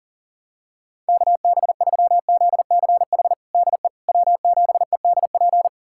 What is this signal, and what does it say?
Morse code, heard as a keyed tone, about 40 wpm, 700 Hz. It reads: KB3ZCH DE W7EDP
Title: USA Callsign DE Callsign 40wpm